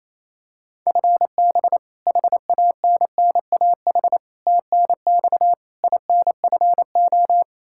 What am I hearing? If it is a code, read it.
Morse code, heard as a keyed tone, about 28 wpm, 700 Hz. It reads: FB HANNAH TNX INFO
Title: Practice QSO - Exchange 3 - More Info - 28wpm